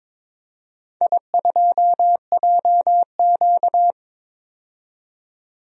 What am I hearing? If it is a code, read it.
Morse code, heard as a keyed tone, about 22 wpm, 700 Hz. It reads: I2JQ
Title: DXCC-100 Call Signs - 4 Characters - 22wpm